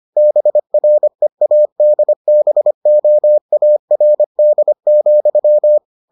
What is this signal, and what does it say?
Morse code, heard as a keyed tone, about 25 wpm, 600 Hz. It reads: BREADBOARD,